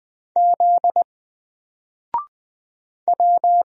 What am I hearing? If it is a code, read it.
Morse code, heard as a keyed tone, about 20 wpm, 700 Hz. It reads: Z W
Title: Single Letter-Number Alt 20wpm